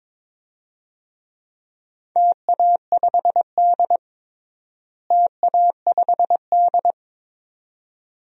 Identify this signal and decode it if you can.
Morse code, heard as a keyed tone, about 22 wpm, 700 Hz. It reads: TA5D TA5D
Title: DXCC-100 Call Signs - 4 Characters - Repeated 2x - 22wpm